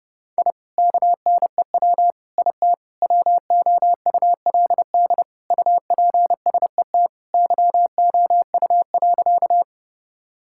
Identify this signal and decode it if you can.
Morse code, heard as a keyed tone, about 30 wpm, 700 Hz. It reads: I KNEW IT WOULD UPSET YOU.
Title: Sentences from Top 1000 Words - Review Part 5 - 30wpm